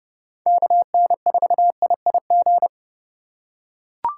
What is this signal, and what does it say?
Morse code, heard as a keyed tone, about 30 wpm, 700 Hz. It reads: KN4IIG